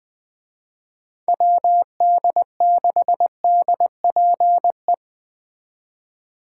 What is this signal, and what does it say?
Morse code, heard as a keyed tone, about 20 wpm, 700 Hz. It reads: WD6DPE